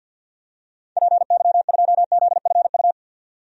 Morse code, heard as a keyed tone, about 48 wpm, 700 Hz. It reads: PY2ZFU
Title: DXCC-100 Call Signs - 6 Characters - 50wpm